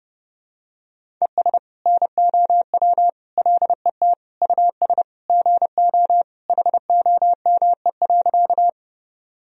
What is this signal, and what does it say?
Morse code, heard as a keyed tone, about 30 wpm, 700 Hz. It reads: ES NOW LET US GO HOME.
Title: Sentences from Top 300 Words and Abbreviations 30wpm